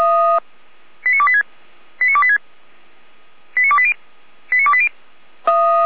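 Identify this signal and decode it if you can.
Five-tone selective call (ZVEI), heard as quick runs of five tones, tones of about 70 ms. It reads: 89286 89286 89280 89280
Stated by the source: ZVEI five-tone selective call received on an SDR, logged by Feuerwehr Orlamünde fire brigade.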